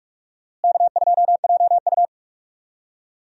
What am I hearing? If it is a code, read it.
Morse code, heard as a keyed tone, about 46 wpm, 700 Hz. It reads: K2JU
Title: USA Call Signs - 4 Characters - Repeated 2x - 45wpm